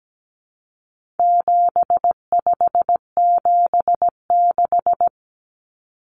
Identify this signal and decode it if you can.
Morse code, heard as a keyed tone, about 17 wpm, 700 Hz. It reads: 7576
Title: Common Cut Numbers - 4-Digit 17wpm